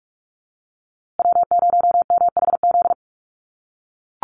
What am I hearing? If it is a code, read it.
Morse code, heard as a keyed tone, about 44 wpm, 700 Hz. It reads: W0MHZ
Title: USA Call Signs Repeated 3x 45wpm